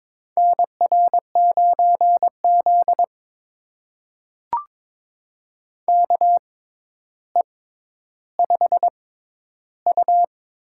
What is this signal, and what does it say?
Morse code, heard as a keyed tone, about 22 wpm, spaced as for 11 wpm, 700 Hz. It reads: NR9Z KE5U
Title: USA Call Signs - 4 Characters - 6x Character Spacing - 22wpm